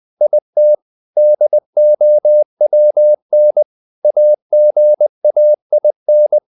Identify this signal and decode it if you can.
Morse code, heard as a keyed tone, about 20 wpm, 600 Hz. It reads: IT DOWN AGAIN
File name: the old man and the sea part 2 20wpm